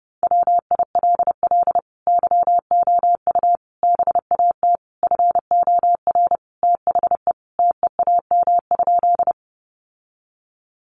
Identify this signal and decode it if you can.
Morse code, heard as a keyed tone, about 30 wpm, 700 Hz. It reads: WILL YOU BAT FOR THE TEAM?
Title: Sentences from Top 1500 Words 30wpm